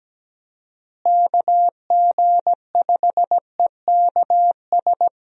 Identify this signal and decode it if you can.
Morse code, heard as a keyed tone, about 17 wpm, 700 Hz. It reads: KG5EKS
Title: USA Call Signs - 6 Characters - Repeated 2x - 17wpm